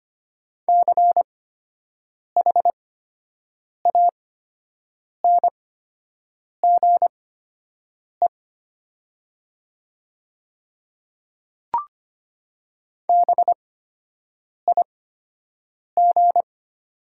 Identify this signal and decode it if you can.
Morse code, heard as a keyed tone, about 25 wpm, spaced as for 7 wpm, 700 Hz. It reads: CHANGE BIG
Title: 8x Character Spacing - Top 200 Words - 25wpm